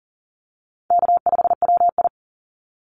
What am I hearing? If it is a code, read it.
Morse code, heard as a keyed tone, about 40 wpm, 700 Hz. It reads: K5WI
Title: NA SOTA Chaser Callsigns 40wpm